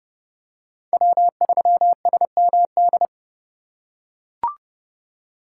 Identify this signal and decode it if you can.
Morse code, heard as a keyed tone, about 27 wpm, 700 Hz. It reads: W3SMD